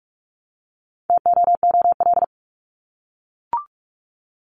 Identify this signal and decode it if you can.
Morse code, heard as a keyed tone, about 44 wpm, 700 Hz. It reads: TOOL
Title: Top 751-1000 Words 45wpm